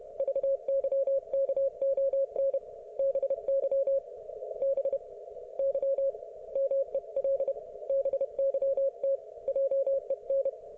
Morse code, heard as a keyed tone, about 31 wpm, spaced as for 26 wpm, 530 Hz. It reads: VYKON BY BY MEL BYT JEN